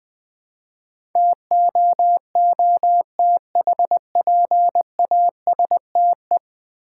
Morse code, heard as a keyed tone, about 20 wpm, 700 Hz. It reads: TOOTHPASTE
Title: Easy Words with Hints - 20wpm